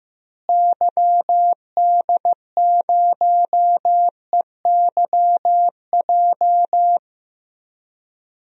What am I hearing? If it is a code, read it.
Morse code, heard as a keyed tone, about 15 wpm, 700 Hz. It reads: YD0EYJ
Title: DXCC-100 Call Signs - 6 Characters - Repeated 2x - 15wpm